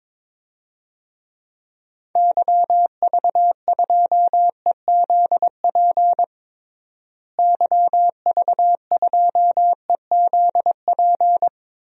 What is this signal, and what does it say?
Morse code, heard as a keyed tone, about 22 wpm, 700 Hz. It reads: YV2EZP YV2EZP
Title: DXCC-100 Call Signs - 6 Characters - Repeated 3x - 22wpm